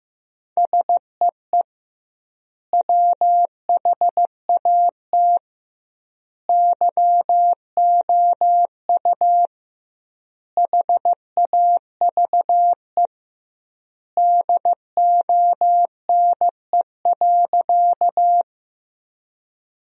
Morse code, heard as a keyed tone, about 15 wpm, 700 Hz. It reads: SEE WHAT YOU HAVE DONE.
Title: 2x Word-Spacing - Sentences from Top 500 Words - 15wpm